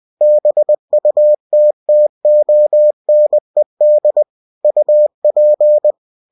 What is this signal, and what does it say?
Morse code, heard as a keyed tone, about 20 wpm, 600 Hz. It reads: BUTTONED UP
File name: cw-books / winnie-the-pooh / winnie-20wpm-page-0063